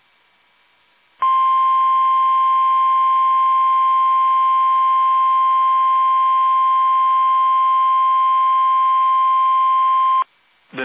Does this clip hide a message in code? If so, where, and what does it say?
Yes: Emergency Alert System (SAME) tones (the 1050 Hz attention tone) from 1.2 to 10.3 s.